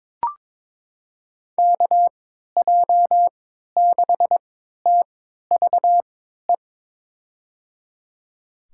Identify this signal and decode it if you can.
Morse code, heard as a keyed tone, about 22 wpm, spaced as for 12 wpm, 700 Hz. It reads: KJ6TVE
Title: USA Call Signs - 6 Characters - 3x Character Spacing - 22wpm